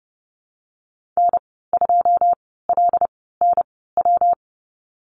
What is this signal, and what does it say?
Morse code, heard as a keyed tone, about 30 wpm, spaced as for 17 wpm, 700 Hz. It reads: N2LNW
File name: Lesson 19g - Letter w - Call Signs 30wpm